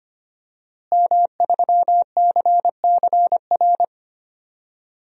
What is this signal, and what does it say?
Morse code, heard as a keyed tone, about 25 wpm, 700 Hz. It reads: M3CCR
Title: DXCC-100 Call Signs - 5 Characters - Repeated 2x - 25wpm